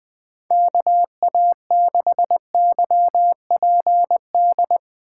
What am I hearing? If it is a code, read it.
Morse code, heard as a keyed tone, about 20 wpm, 700 Hz. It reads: KA6YPD
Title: USA Call Signs - 6 Characters - 2x Character Spacing - 20wpm